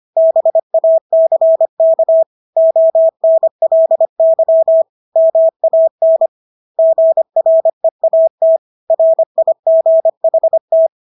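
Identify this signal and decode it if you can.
Morse code, heard as a keyed tone, about 25 wpm, 650 Hz. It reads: BACK ONLY MAN GREAT RIGHT
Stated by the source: Morse practice of English words at 25 wpm from CW med Gnister.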